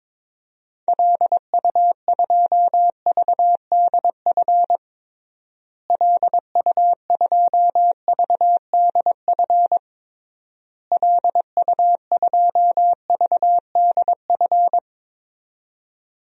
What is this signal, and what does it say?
Morse code, heard as a keyed tone, about 22 wpm, 700 Hz. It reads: LU2VDF LU2VDF LU2VDF
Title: DXCC-100 Call Signs - 6 Characters - Repeated 3x - 22wpm